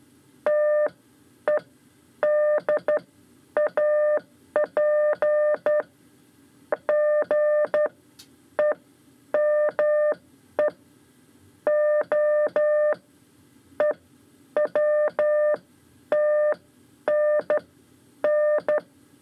Morse code, heard as a keyed tone, about 12 wpm, spaced as for 8 wpm, 600 Hz. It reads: TEDAP GEME OEWTNN
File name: 5 Lezione CW 2